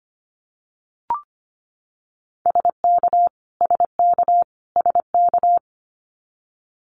Morse code, heard as a keyed tone, about 25 wpm, 700 Hz. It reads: SK SK SK
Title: Canadian Provinces and Territories - Repeated 3x - 25wpm